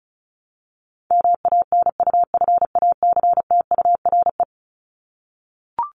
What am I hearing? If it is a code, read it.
Morse code, heard as a keyed tone, about 34 wpm, 700 Hz. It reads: MANUFACTURE
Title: ING Suffix - Top 100 Words with Base Words 35wpm